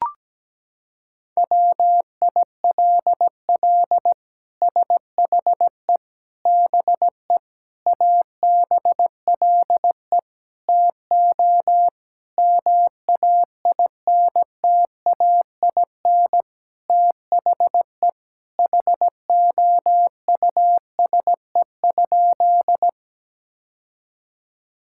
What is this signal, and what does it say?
Morse code, heard as a keyed tone, about 17 wpm, 700 Hz. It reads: WILL SHE BE ABLE TO MAINTAIN THE HOUSE?
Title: Sentences from Top 1000 Words - Review Part 5 - 17wpm